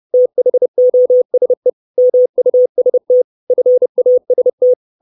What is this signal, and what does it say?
Morse code, heard as a keyed tone, about 30 wpm, 500 Hz. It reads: THOSE MUST FAST
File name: 632 Words 30WPM